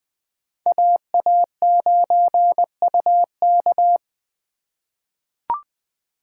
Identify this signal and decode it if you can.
Morse code, heard as a keyed tone, about 20 wpm, 700 Hz. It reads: AA9UK
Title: USA Call Signs - 5 Characters - 3x Character Spacing - 20wpm